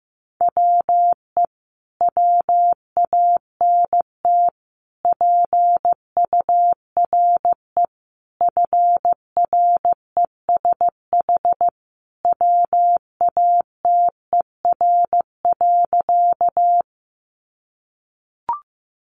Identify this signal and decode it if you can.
Morse code, heard as a keyed tone, about 15 wpm, 700 Hz. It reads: WE WANT PURE FRESH WATER.
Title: Sentences from Top 1500 Words 15wpm